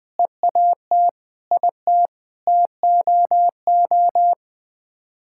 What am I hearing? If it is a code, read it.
Morse code, heard as a keyed tone, about 21 wpm, 700 Hz. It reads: EAT IT TOO
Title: Sets of 9 Words - 20wpm